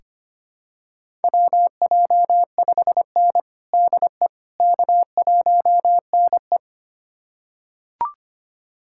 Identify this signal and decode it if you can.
Morse code, heard as a keyed tone, about 25 wpm, 700 Hz. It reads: WJ5N DE K1NE
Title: USA Callsign DE Callsign - 4 Characters - 25wpm